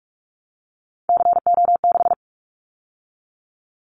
Morse code, heard as a keyed tone, about 44 wpm, 700 Hz. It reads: CO6
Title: Callsign Prefixes for Top 100 DXCC - 45wpm